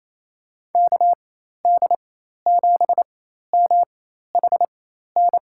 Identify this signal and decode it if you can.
Morse code, heard as a keyed tone, about 27 wpm, spaced as for 13 wpm, 700 Hz. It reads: KD7MHN
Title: USA Call Signs - 6 Characters - 4x Character Spacing - 28wpm